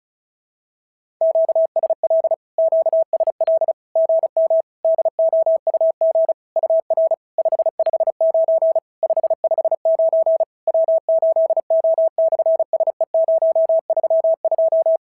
Morse code, heard as a keyed tone, about 33 wpm, 650 Hz. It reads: QSL QSL GM DOUG UR 559 559 W8O/SE032
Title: SOTA QSO Practice 35wpm